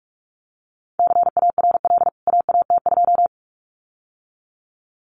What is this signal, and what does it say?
Morse code, heard as a keyed tone, about 44 wpm, 700 Hz. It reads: CARL AAT2
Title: CWT Exchange 45wpm